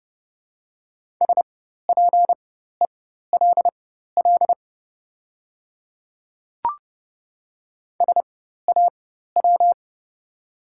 Morse code, heard as a keyed tone, about 30 wpm, spaced as for 14 wpm, 700 Hz. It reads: SPELL SAW
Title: 4x Character Spacing - Top 500 Words - 30wpm